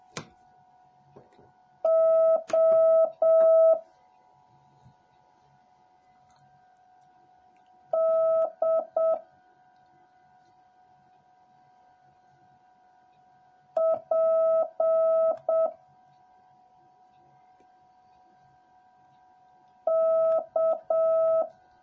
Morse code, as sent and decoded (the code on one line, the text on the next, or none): --- / -.. / .--. / -.-
O D P K